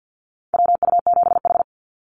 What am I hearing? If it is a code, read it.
Morse code, heard as a keyed tone, about 50 wpm, 700 Hz. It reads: WU7H
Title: NA SOTA Activator Callsigns 50wpm